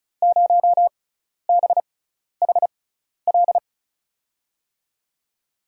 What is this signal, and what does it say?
Morse code, heard as a keyed tone, about 34 wpm, spaced as for 12 wpm, 700 Hz. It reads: 0BHL
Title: USA Call Signs - 6 Characters - 6x Character Spacing - 35wpm